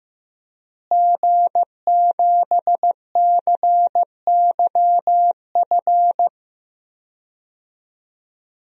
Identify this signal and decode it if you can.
Morse code, heard as a keyed tone, about 15 wpm, 700 Hz. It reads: G7CYF
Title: DXCC-100 Call Signs - 5 Characters - Repeated 2x - 15wpm